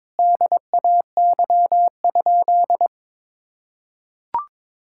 Morse code, heard as a keyed tone, about 22 wpm, 700 Hz. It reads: DAY?
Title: Sentences from Top 1000 Words - Review Part 6 - 22wpm